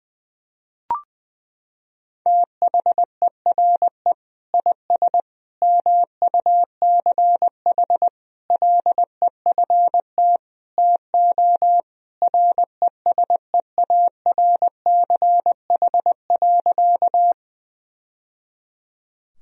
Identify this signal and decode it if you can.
Morse code, heard as a keyed tone, about 20 wpm, 700 Hz. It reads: THERE IS MUCH LEFT TO RESEARCH.
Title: Sentences from Top 1000 Words - Review Part 4 - 20wpm